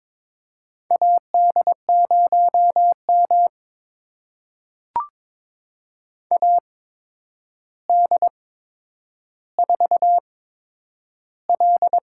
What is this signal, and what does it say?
Morse code, heard as a keyed tone, about 22 wpm, spaced as for 9 wpm, 700 Hz. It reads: AD0M AD4L
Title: USA Call Signs - 4 Characters - 8x Character Spacing - 22wpm